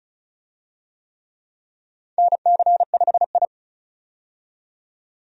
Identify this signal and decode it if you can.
Morse code, heard as a keyed tone, about 36 wpm, 700 Hz. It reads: NC5I
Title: USA Call Signs - 4 Characters - 35wpm